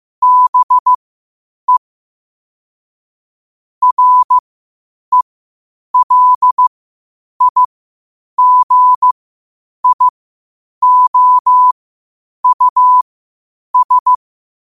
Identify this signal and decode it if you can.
Morse code, heard as a keyed tone, about 15 wpm, spaced as for 8 wpm, 1000 Hz. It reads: BE RELIGIOUS
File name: qotdfiles / 8wpm